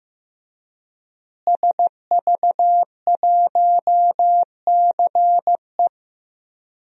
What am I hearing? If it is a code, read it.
Morse code, heard as a keyed tone, about 15 wpm, 700 Hz. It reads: SV1CE
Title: DXCC-100 Call Signs - 5 Characters - Repeated 2x - 15wpm